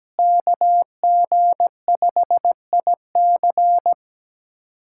Morse code, heard as a keyed tone, about 17 wpm, 700 Hz. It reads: KG5IC
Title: USA Callsign DE Callsign - 5 Characters - 17wpm